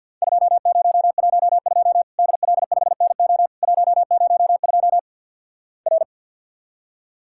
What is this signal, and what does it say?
Morse code, heard as a keyed tone, about 48 wpm, 700 Hz. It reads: 2012 DL5NO 102 R
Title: Worked All Europe DX Contest - QTC Exchange with Repeats - 50wpm